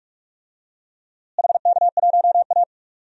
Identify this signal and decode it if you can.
Morse code, heard as a keyed tone, about 43 wpm, 700 Hz. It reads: HK1A
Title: DXCC-100 Call Signs - 4 Characters - Repeated 3x - 45wpm